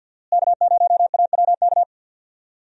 Morse code, heard as a keyed tone, about 46 wpm, 700 Hz. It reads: K0AWK
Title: USA Call Signs - 5 Characters - Repeated 3x - 50wpm